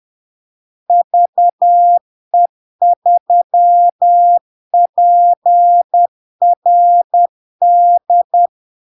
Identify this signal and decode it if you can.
Morse code, heard as a keyed tone, about 10 wpm, 700 Hz. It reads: VE3PRD
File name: LARK-FD-2022-Log-in-CW-10wpm-700-Hz